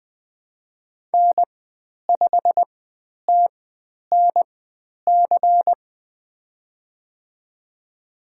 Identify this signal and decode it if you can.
Morse code, heard as a keyed tone, about 20 wpm, spaced as for 10 wpm, 700 Hz. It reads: N5TNC
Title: 095 Lesson 16g - Letter c - Call Signs 20wpm